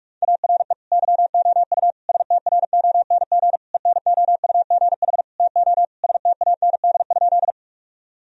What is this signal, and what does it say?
Morse code, heard as a keyed tone, about 46 wpm, 700 Hz. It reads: ARE YOU STRONG ENOUGH TO STAND?
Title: Sentences from Top 1000 Words - Review Part 3 - 45wpm